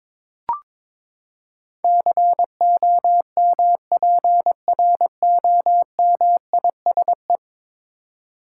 Morse code, heard as a keyed tone, about 22 wpm, 700 Hz. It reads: COMPROMISE
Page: morse-code-ninja-practice-22wpm